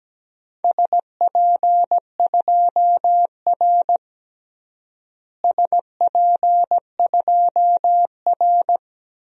Morse code, heard as a keyed tone, about 17 wpm, 700 Hz. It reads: SP2R SP2R
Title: DXCC-100 Call Signs - 4 Characters - Repeated 2x - 17wpm